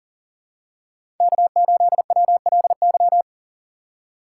Morse code, heard as a keyed tone, about 40 wpm, 700 Hz. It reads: K8WLY
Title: USA Call Signs - 5 Characters - 4x Character Spacing - 40wpm